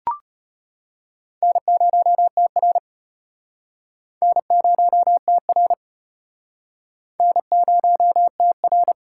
Morse code, heard as a keyed tone, about 36 wpm, 700 Hz. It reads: N0TR N0TR N0TR